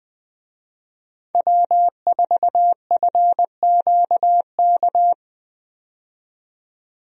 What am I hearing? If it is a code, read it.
Morse code, heard as a keyed tone, about 20 wpm, 700 Hz. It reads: W4FQK
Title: USA Call Signs - 5 Characters - 20wpm